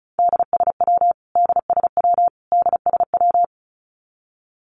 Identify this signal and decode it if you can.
Morse code, heard as a keyed tone, about 34 wpm, 700 Hz. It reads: DSW DSW DSW
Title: Abbreviations Repeated 3x - 35wpm